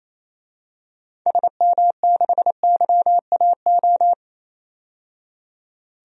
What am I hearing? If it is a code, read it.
Morse code, heard as a keyed tone, about 28 wpm, 700 Hz. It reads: SM6YAO
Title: DXCC-100 Call Signs - 6 Characters - 28wpm